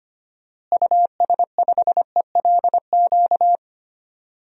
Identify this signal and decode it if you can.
Morse code, heard as a keyed tone, about 25 wpm, 700 Hz. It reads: US5ELQ
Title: DXCC-100 Call Signs - 6 Characters - Repeated 2x - 25wpm